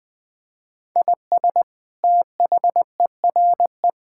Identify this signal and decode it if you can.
Morse code, heard as a keyed tone, about 21 wpm, 700 Hz. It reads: IS THERE
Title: Sentences from Top 1000 Words - Review Part 5 - 20wpm